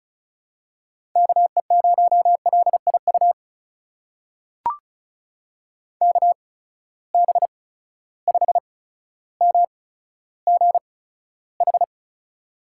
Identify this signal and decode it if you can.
Morse code, heard as a keyed tone, about 36 wpm, spaced as for 15 wpm, 700 Hz. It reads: KE0LIU KB5MGH
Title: USA Call Signs - 6 Characters - 8x Character Spacing - 35wpm